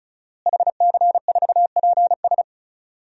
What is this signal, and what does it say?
Morse code, heard as a keyed tone, about 36 wpm, 700 Hz. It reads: HC4PS
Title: DXCC-100 Call Signs - 5 Characters - Repeated 3x - 35wpm